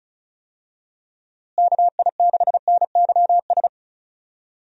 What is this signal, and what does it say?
Morse code, heard as a keyed tone, about 34 wpm, 700 Hz. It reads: KI6NYS
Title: USA Call Signs - 6 Characters - Repeated 3x - 35wpm